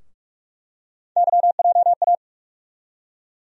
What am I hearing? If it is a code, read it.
Morse code, heard as a keyed tone, about 46 wpm, 700 Hz. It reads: YJA
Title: USA Call Sign Postfixes - 3 Letters 45wpm